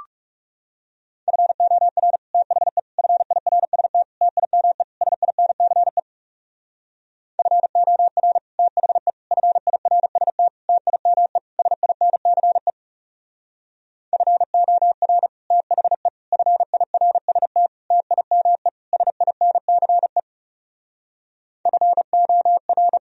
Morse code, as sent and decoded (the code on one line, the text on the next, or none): ..-. --- .-. / - .... / ..-. .. .-. ... - / - .. -- . / ... .. -. -.-. / ..-. --- .-. / - .... / ..-. .. .-. ... - / - .. -- . / ... .. -. -.-. / ..-. --- .-. / - .... / ..-. .. .-. ... - / - .. -- . / ... .. -. -.-. / ..-. --- .-.
FOR THE FIRST TIME SINCE FOR THE FIRST TIME SINCE FOR THE FIRST TIME SINCE FOR